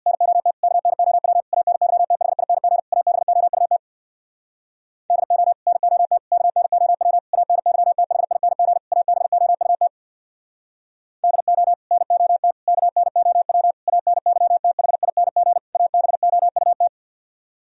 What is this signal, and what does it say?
Morse code, heard as a keyed tone, about 64 wpm, 700 Hz. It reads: NOT KNOW ANYTHING ABOUT DO NOT KNOW ANYTHING ABOUT DO NOT KNOW ANYTHING ABOUT